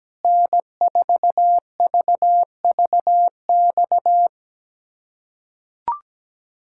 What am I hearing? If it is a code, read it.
Morse code, heard as a keyed tone, about 17 wpm, 700 Hz. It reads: N4VVX
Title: USA Callsign DE Callsign 17wpm